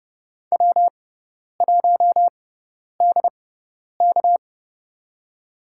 Morse code, heard as a keyed tone, about 30 wpm, spaced as for 10 wpm, 700 Hz. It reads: W1DK
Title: USA Call Signs - 4 Characters - 6x Character Spacing - 30wpm